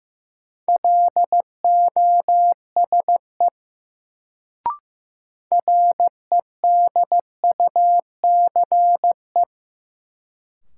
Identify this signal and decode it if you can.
Morse code, heard as a keyed tone, about 15 wpm, 700 Hz. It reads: LOSE REDUCE
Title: Sentences from Top 1100 Words - New Word Drill Alt - 15wpm